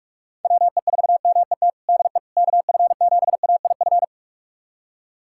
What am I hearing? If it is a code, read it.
Morse code, heard as a keyed tone, about 46 wpm, 700 Hz. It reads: WE4MET DE KF7AIF
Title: USA Callsign DE Callsign - 6 Characters - 45wpm